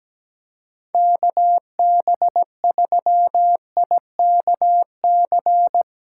Morse code, heard as a keyed tone, about 17 wpm, 700 Hz. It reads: KB3IKC
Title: USA Callsign DE Callsign - 6 Characters - 17wpm